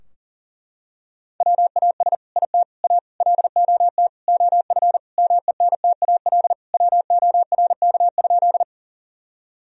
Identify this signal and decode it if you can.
Morse code, heard as a keyed tone, about 40 wpm, 700 Hz. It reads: WAS IT A LOT OF MENTAL WORK?
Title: Sentences from Top 1000 Words - Review Part 4 - 40wpm